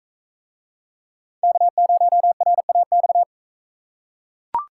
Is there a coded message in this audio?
Yes: Morse code from 1.4 to 3.2 s, heard as a keyed tone, about 43 wpm, 700 Hz. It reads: K0RAX